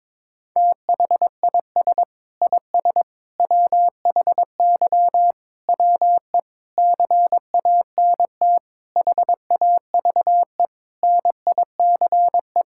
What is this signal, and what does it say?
Morse code, heard as a keyed tone, about 22 wpm, 700 Hz. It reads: THIS IS WHY WE CANT HAVE NICE
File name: practice / 22wpm / Sentences with 8-Word Phrases - 22wpm